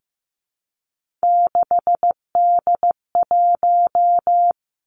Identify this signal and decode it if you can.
Morse code, heard as a keyed tone, about 15 wpm, 700 Hz. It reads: 6D1
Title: Three Challenging Letter-Number Combinations 15wpm